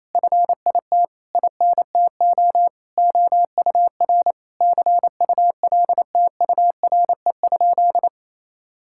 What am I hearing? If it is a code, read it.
Morse code, heard as a keyed tone, about 28 wpm, 700 Hz. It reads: FIT INTO OUR CULTURE?
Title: Sentences from Top 1000 Words - Review Part 6 - 28wpm